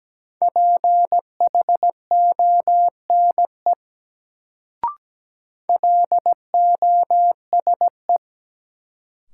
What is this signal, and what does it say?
Morse code, heard as a keyed tone, about 17 wpm, 700 Hz. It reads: PHONE LOSE
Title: Sentences from Top 1100 Words - New Word Drill Alt - 17wpm